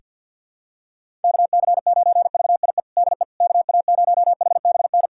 Morse code, heard as a keyed tone, about 53 wpm, 700 Hz. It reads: KC9VIE DE KA0HBN
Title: USA Callsign DE Callsign - 6 Characters - 50wpm